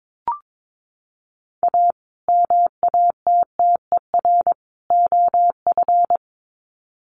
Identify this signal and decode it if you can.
Morse code, heard as a keyed tone, about 22 wpm, 700 Hz. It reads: A MATTER OF
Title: Sets of 3 Words 22wpm